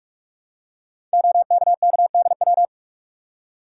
Morse code, heard as a keyed tone, about 44 wpm, 700 Hz. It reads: OKKDW
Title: Code Groups - 5 Letter:Numbers - 45wpm